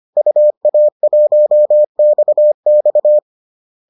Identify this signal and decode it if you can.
Morse code, heard as a keyed tone, about 25 wpm, 600 Hz. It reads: UA1XX